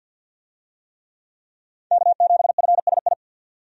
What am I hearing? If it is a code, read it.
Morse code, heard as a keyed tone, about 51 wpm, 700 Hz. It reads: K7FSI
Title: USA Call Signs - 5 Characters - 50wpm